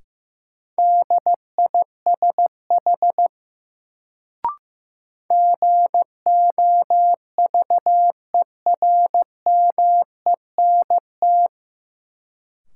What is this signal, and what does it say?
Morse code, heard as a keyed tone, about 15 wpm, 700 Hz. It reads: DISH GOVERMENT
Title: Sentences from Top 1300 Words - New Word Drill - 15wpm